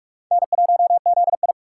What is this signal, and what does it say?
Morse code, heard as a keyed tone, about 46 wpm, 700 Hz. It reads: N1ZI
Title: USA Call Signs - 2x Character Spacing - 45wpm